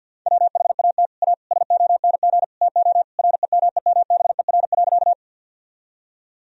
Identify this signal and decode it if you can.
Morse code, heard as a keyed tone, about 48 wpm, 700 Hz. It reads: WHAT A SONG TO REMEMBER.
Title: Sentences from Top 1000 Words - Review Part 2 - 50wpm